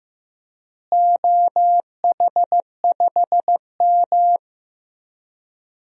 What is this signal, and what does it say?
Morse code, heard as a keyed tone, about 15 wpm, 700 Hz. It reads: OH5M
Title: DXCC-100 Call Signs - 4 Characters - Repeated 3x - 15wpm